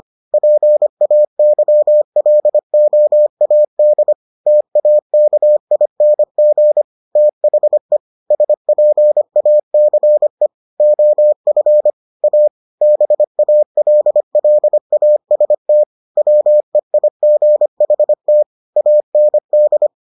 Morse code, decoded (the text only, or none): PAYLOAD TAKING THE SPACE OF A BALLAST WEIGHT AND